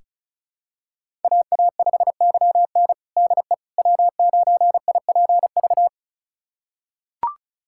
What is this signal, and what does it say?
Morse code, heard as a keyed tone, about 36 wpm, 700 Hz. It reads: AA5YN DE W9IPV
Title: USA Callsign DE Callsign - 5 Characters - 35wpm